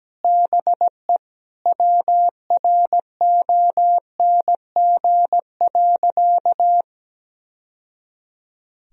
Morse code, heard as a keyed tone, about 17 wpm, 700 Hz. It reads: BE WRONG.
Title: Sentences from Top 1000 Words - Review Part 3 - 17wpm